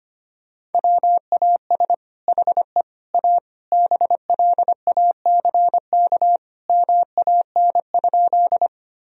Morse code, heard as a keyed tone, about 25 wpm, 700 Hz. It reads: WAS HE A BLACK MAN?
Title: Sentences from Top 1000 Words - Review Part 2 - 25wpm